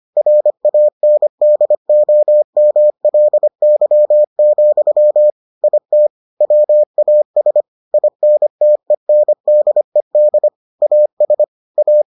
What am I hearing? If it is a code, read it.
Morse code, heard as a keyed tone, about 25 wpm, 600 Hz. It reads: RANDOMLY, IT WAS INTENDED AS A